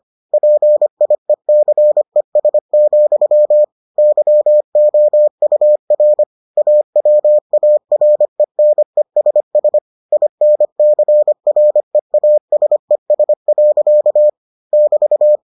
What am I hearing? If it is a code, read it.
Morse code, heard as a keyed tone, about 25 wpm, 600 Hz. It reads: PIECES, YOUR AWARENESS INCREASES. <BT>